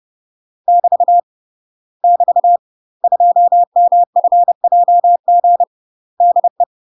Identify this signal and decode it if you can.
Morse code, heard as a keyed tone, about 30 wpm, 700 Hz. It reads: <BT> <BT> 2MFJG DE
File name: UK MASTER QSOs TRAINING / 30WPM / UK MASTER QSOs 0039